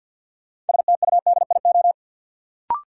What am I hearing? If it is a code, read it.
Morse code, heard as a keyed tone, about 47 wpm, 700 Hz. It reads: STUDIO
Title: Sentences from Top 1200 Words - New Word Drill - 50wpm